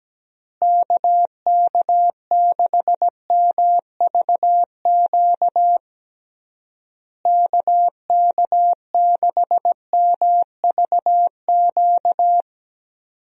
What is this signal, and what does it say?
Morse code, heard as a keyed tone, about 17 wpm, 700 Hz. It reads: KK6MVQ KK6MVQ
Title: USA Call Signs - 6 Characters - Repeated 3x - 17wpm